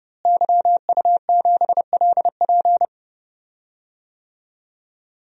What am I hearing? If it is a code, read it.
Morse code, heard as a keyed tone, about 30 wpm, 700 Hz. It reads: YU7LP
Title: DXCC-100 Call Signs - 5 Characters - Repeated 2x - 30wpm